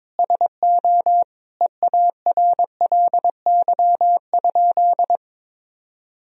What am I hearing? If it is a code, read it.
Morse code, heard as a keyed tone, about 22 wpm, 700 Hz. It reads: SO EARLY?
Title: Sentences from Top 1000 Words - Review Part 2 - 22wpm